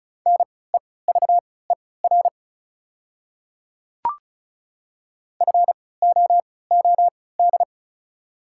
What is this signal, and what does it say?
Morse code, heard as a keyed tone, about 34 wpm, spaced as for 20 wpm, 690 Hz. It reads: NEVER FOOD
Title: 3x Character Spacing - Top 500 Words - 35wpm